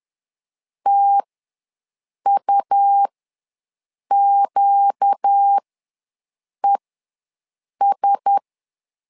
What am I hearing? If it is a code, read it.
Morse code, heard as a keyed tone, about 10 wpm, spaced as for 6 wpm, 800 Hz. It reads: TUQES
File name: LEZIONI / 4 Q1S ws 10wpm 2208